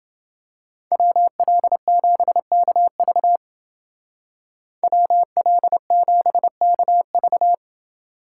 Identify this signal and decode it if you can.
Morse code, heard as a keyed tone, about 28 wpm, 700 Hz. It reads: WL7KV WL7KV